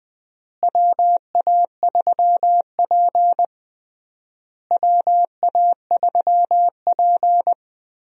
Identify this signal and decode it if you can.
Morse code, heard as a keyed tone, about 20 wpm, 700 Hz. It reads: WA3P WA3P
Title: USA Call Signs - 4 Characters - Repeated 3x - 20wpm